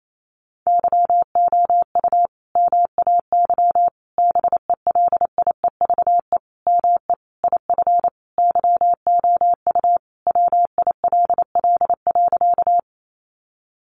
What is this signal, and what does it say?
Morse code, heard as a keyed tone, about 28 wpm, 700 Hz. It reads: YOU MAY BELIEVE ME IF YOU WILL.
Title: Sentences from Top 1000 Words 28wpm